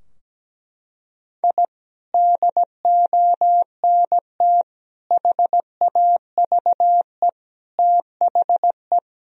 Morse code, heard as a keyed tone, about 17 wpm, 700 Hz. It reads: I DONT HAVE THE
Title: Sentences from Top 1000 Words - Review Part 3 - 17wpm